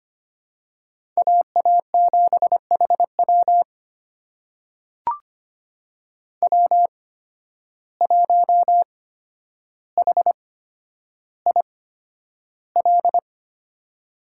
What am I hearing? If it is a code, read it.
Morse code, heard as a keyed tone, about 25 wpm, spaced as for 11 wpm, 700 Hz. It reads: AA7HW W1HIL
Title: USA Call Signs - 5 Characters - 8x Character Spacing - 25wpm